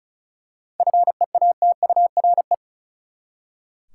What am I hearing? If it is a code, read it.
Morse code, heard as a keyed tone, about 36 wpm, 700 Hz. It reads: FEATURE
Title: Seven Letter Words - 35wpm